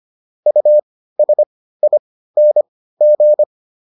Morse code, heard as a keyed tone, about 25 wpm, spaced as for 15 wpm, 600 Hz. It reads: USING